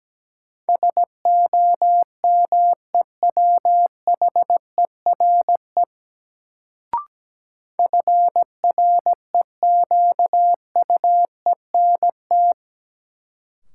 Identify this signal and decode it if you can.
Morse code, heard as a keyed tone, about 17 wpm, 700 Hz. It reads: SOMEWHERE FREQUENT
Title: Sentences from Top 1400 Words - New Word Drill Alt - 17wpm